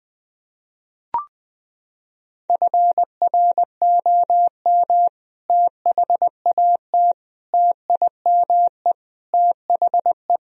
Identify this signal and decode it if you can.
Morse code, heard as a keyed tone, about 20 wpm, 700 Hz. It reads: FROM THAT TIME THE
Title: Sentences from Top 1000 Words - Review Part 1 - 20wpm